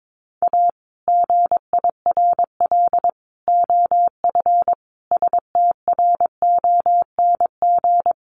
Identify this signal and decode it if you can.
Morse code, heard as a keyed tone, about 22 wpm, 700 Hz. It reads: A GIRL OF STRONG
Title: Sentences from Top 1200 Words 22wpm